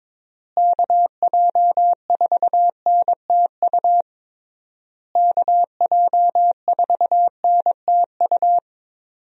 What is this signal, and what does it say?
Morse code, heard as a keyed tone, about 22 wpm, 700 Hz. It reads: KJ4NTU KJ4NTU
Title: USA Call Signs - 6 Characters - Repeated 3x - 22wpm